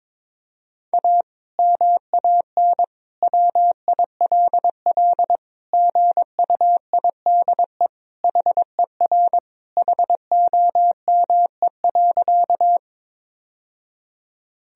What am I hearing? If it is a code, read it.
Morse code, heard as a keyed tone, about 22 wpm, 700 Hz. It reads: A MAN WILL GUIDE HER HOME.
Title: Sentences from Top 1000 Words - Review Part 3 - 22wpm